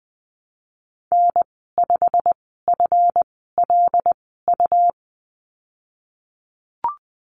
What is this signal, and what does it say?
Morse code, heard as a keyed tone, about 20 wpm, spaced as for 15 wpm, 700 Hz. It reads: N5FLU
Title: 219 Lesson 35g - Number 0 - Call Signs 20wpm